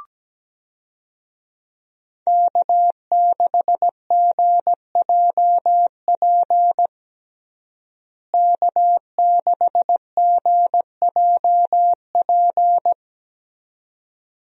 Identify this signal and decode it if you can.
Morse code, heard as a keyed tone, about 17 wpm, 700 Hz. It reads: K6GJP K6GJP
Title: USA Call Signs - 5 Characters - Repeated 2x - 17wpm